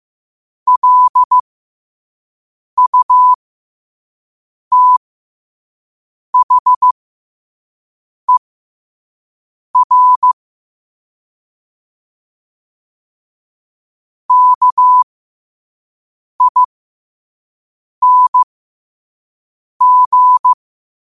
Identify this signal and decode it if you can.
Morse code, heard as a keyed tone, about 15 wpm, spaced as for 5 wpm, 1000 Hz. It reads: LUTHER KING
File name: qotdfiles / 5wpm